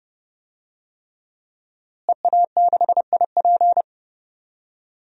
Morse code, heard as a keyed tone, about 32 wpm, 700 Hz. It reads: EA6IP
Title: DXCC-100 Call Signs - 5 Characters - Repeated 3x - 30wpm